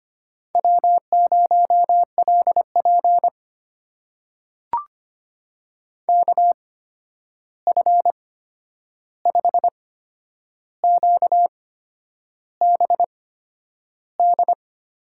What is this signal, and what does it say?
Morse code, heard as a keyed tone, about 25 wpm, spaced as for 9 wpm, 700 Hz. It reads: W0LP KF5QBD
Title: USA Call Signs - 8x Character Spacing - 25wpm